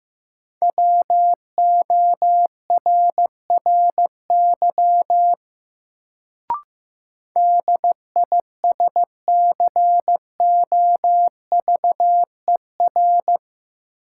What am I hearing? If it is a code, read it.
Morse code, heard as a keyed tone, about 15 wpm, 700 Hz. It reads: WORRY DISCOVER
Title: Sentences from Top 1100 Words - New Word Drill - 15wpm